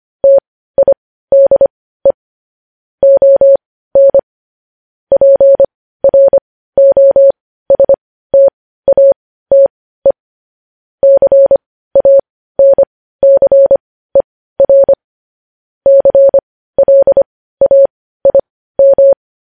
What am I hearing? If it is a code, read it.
Morse code, heard as a keyed tone, about 25 wpm, spaced as for 15 wpm, 550 Hz. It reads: TIDE ON PROSTATE CANCER CLAIM
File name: podcast / News headlines-1x-25x15